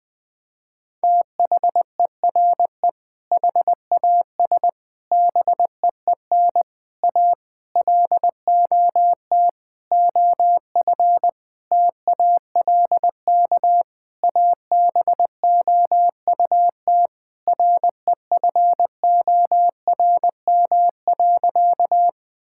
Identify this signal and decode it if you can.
Morse code, heard as a keyed tone, about 20 wpm, 700 Hz. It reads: THERE HAS BEEN A LOT OF TALK ABOUT REFORM.
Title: Sentences with 7-Word Phrases - Spoken Beginning - 20wpm